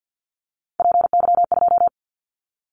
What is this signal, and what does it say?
Morse code, heard as a keyed tone, about 50 wpm, 700 Hz. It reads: PY2
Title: Callsign Prefixes for Top 100 DXCC - 50wpm